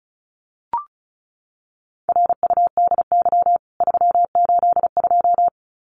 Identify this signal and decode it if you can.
Morse code, heard as a keyed tone, about 34 wpm, 700 Hz. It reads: RUDY 382
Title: CWT Exchange 35wpm